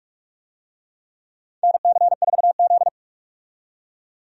Morse code, heard as a keyed tone, about 47 wpm, 700 Hz. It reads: NC4Z
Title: USA Call Signs - 4 Characters - 45wpm